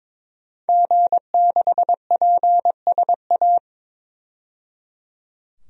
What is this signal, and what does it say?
Morse code, heard as a keyed tone, about 22 wpm, 700 Hz. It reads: G6PSA
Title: DXCC-100 Call Signs - 5 Characters - 22wpm